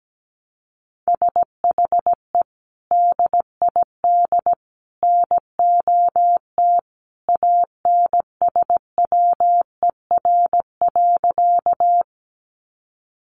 Morse code, heard as a keyed tone, about 17 wpm, 700 Hz. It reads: SHE DID NOT ANSWER.
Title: Sentences from Top 600 Words 17wpm